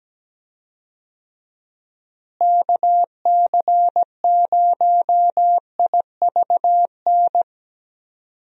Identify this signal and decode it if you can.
Morse code, heard as a keyed tone, about 17 wpm, 700 Hz. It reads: KC0IVN
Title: USA Call Signs - 6 Characters - Repeated 2x - 17wpm